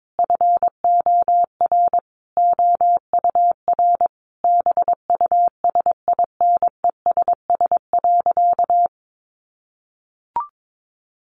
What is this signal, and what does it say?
Morse code, heard as a keyed tone, about 22 wpm, 700 Hz. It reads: FOR OUR BUSINESS.
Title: Sentences from Top 1400 Words 22wpm